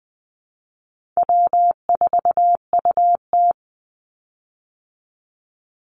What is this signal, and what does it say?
Morse code, heard as a keyed tone, about 20 wpm, 700 Hz. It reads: W4UT
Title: Call Signs Repeated 3x - 20wpm